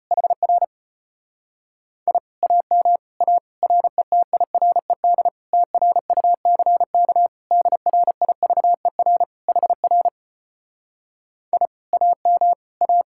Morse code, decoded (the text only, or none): HR I AM A RETIRED TRUCK DRIVER HR I AM A